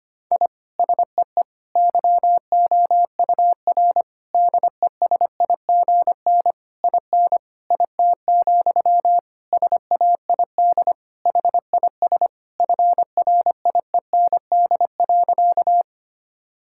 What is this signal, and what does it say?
Morse code, heard as a keyed tone, about 25 wpm, 700 Hz. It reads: I SEE YOUR DESIGN IN IT, SAID HIS FRIEND.
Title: Sentences from Top 1000 Words - Review Part 5 - 25wpm